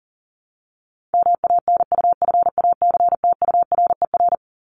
Morse code, heard as a keyed tone, about 40 wpm, 700 Hz. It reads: MANUFACTURER
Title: Top 1751-2000 Words 40wpm